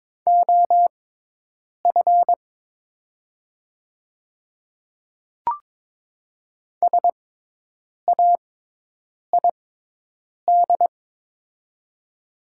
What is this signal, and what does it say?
Morse code, heard as a keyed tone, about 22 wpm, spaced as for 8 wpm, 700 Hz. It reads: OF SAID
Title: 6x Character Spacing - Top 100 Words - 22wpm